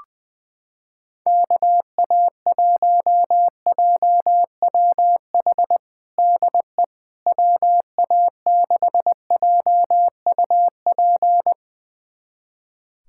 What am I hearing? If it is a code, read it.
Morse code, heard as a keyed tone, about 20 wpm, 700 Hz. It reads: KA1JWH DE WA6JUP
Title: USA Callsign DE Callsign - 6 Characters - 20wpm